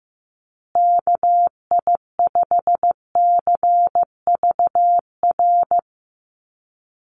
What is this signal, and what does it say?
Morse code, heard as a keyed tone, about 15 wpm, 700 Hz. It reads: KI5CVR
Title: USA Call Signs Repeated 2x 15wpm